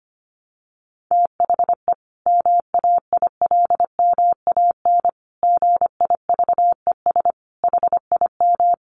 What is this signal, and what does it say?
Morse code, heard as a keyed tone, about 25 wpm, 700 Hz. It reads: THE MAILMAN GIVES HIM
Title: Additional Easy Phrases 25wpm